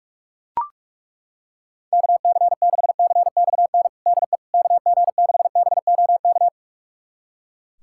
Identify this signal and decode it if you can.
Morse code, heard as a keyed tone, about 42 wpm, 700 Hz. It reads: KC6CXN DE KG6BOK
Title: USA Callsign DE Callsign - 6 Characters - 45wpm